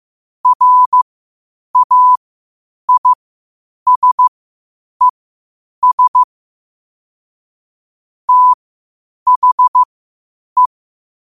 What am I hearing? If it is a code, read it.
Morse code, heard as a keyed tone, about 15 wpm, spaced as for 8 wpm, 1000 Hz. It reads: RAISES THE
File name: qotdfiles / 8wpm